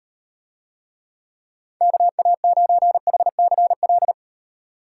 Morse code, heard as a keyed tone, about 34 wpm, 700 Hz. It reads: KA9HCL